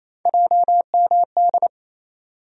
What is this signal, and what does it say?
Morse code, heard as a keyed tone, about 28 wpm, 700 Hz. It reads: JMD
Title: USA Call Sign Postfixes - 3 Letters 28wpm